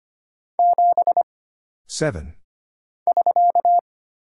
Morse code, heard as a keyed tone, about 25 wpm, 700 Hz. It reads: 7 <SK>
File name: Single Character (Letter:Number:Punctuation:Prosign) - Mind-Melt - 25wpm